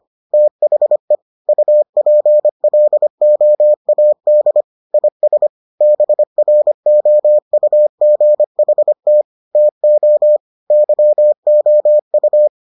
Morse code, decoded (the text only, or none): THE UPLOAD IS BROUGHT TO YOU